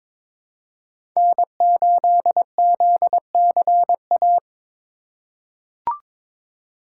Morse code, heard as a keyed tone, about 22 wpm, 700 Hz. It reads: N8ZCA
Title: USA Call Signs - 5 Characters - 4x Character Spacing - 22wpm